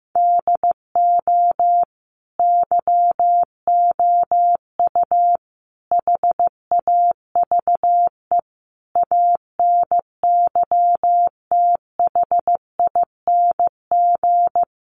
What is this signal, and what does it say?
Morse code, heard as a keyed tone, about 15 wpm, 700 Hz. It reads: DO YOU HAVE ANYTHING
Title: Sentences from Top 1000 Words 15wpm